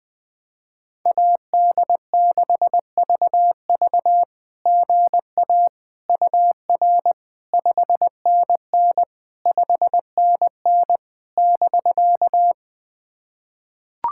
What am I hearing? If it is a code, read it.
Morse code, heard as a keyed tone, about 20 wpm, 700 Hz. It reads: AD6VV GA UR 5NN 5NN <BK>
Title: SOTA QSO Practice 20wpm